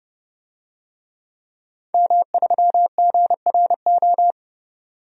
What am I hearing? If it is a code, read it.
Morse code, heard as a keyed tone, about 30 wpm, 700 Hz. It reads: M3GRO
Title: DXCC-100 Call Signs - 5 Characters - 30wpm